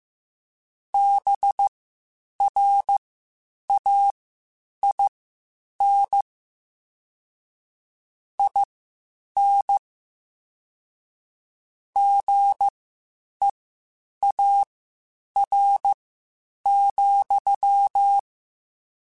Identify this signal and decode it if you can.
Morse code, heard as a keyed tone, about 15 wpm, spaced as for 8 wpm, 800 Hz. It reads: BRAIN IN GEAR,
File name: Spaced / Chuckles Spaced-mp3s / BrainMouth sp @13